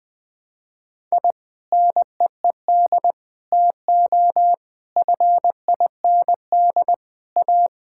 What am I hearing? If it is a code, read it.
Morse code, heard as a keyed tone, about 20 wpm, 700 Hz. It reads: I NEED TO FIND A
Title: Sentences with 3-Word Phrases - 20wpm